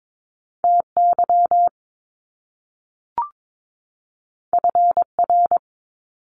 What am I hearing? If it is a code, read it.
Morse code, heard as a keyed tone, about 22 wpm, 700 Hz. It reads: TY FR
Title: Top Two Letter Combinations in Words 22wpm